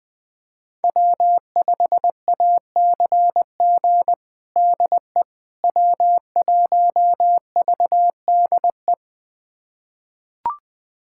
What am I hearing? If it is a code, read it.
Morse code, heard as a keyed tone, about 20 wpm, 700 Hz. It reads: W5ACG DE W1VDE
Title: USA Callsign DE Callsign - 5 Characters - 20wpm